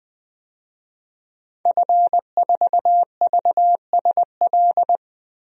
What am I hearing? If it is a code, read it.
Morse code, heard as a keyed tone, about 20 wpm, 700 Hz. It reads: F4VSL
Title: DXCC-100 Call Signs - 5 Characters - Repeated 2x - 20wpm